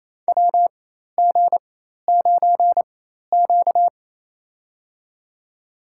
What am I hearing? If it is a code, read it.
Morse code, heard as a keyed tone, about 28 wpm, spaced as for 13 wpm, 700 Hz. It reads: WG9Q
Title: USA Call Signs - 4x Character Spacing - 28wpm